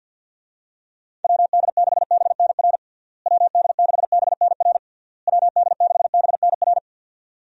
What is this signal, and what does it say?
Morse code, heard as a keyed tone, about 46 wpm, 700 Hz. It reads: WD6BNR WD6BNR WD6BNR
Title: USA Call Signs - 6 Characters - Repeated 3x - 50wpm